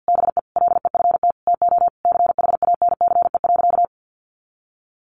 Morse code, heard as a keyed tone, about 50 wpm, 700 Hz. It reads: BE LEFT TO CHANCE.
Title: Sentences from Top 1100 Words 50wpm